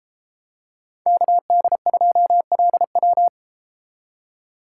Morse code, heard as a keyed tone, about 32 wpm, 700 Hz. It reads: KD2LW